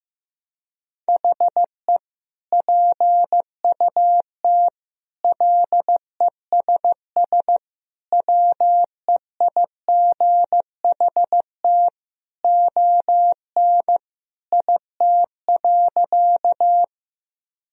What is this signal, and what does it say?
Morse code, heard as a keyed tone, about 15 wpm, 700 Hz. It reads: HE PUT LESS WEIGHT ON IT.
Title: Sentences from Top 1000 Words - Review Part 3 - 15wpm